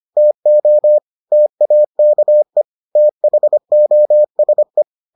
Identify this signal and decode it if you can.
Morse code, heard as a keyed tone, about 25 wpm, 600 Hz. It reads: TO TAKE THOSE